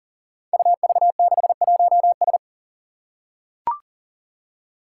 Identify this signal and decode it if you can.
Morse code, heard as a keyed tone, about 40 wpm, 700 Hz. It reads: UV61S
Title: Five Challenging Letter-Number Combinations 40wpm